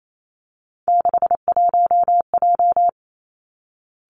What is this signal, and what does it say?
Morse code, heard as a keyed tone, about 28 wpm, 700 Hz. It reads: 61J
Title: Three Challenging Letter-Number Combinations 28wpm